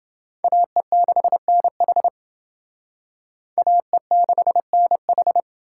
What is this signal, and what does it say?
Morse code, heard as a keyed tone, about 29 wpm, 700 Hz. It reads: AE6NH AE6NH